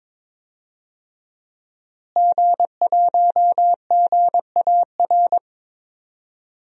Morse code, heard as a keyed tone, about 21 wpm, 700 Hz. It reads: G1GAR
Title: DXCC-100 Call Signs - 5 Characters - Repeated 2x - 22wpm